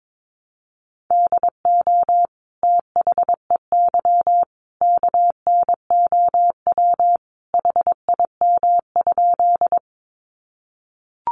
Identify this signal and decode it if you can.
Morse code, heard as a keyed tone, about 22 wpm, 700 Hz. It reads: DO THEY KNOW HIM?
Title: Sentences from Top 100 Words 22wpm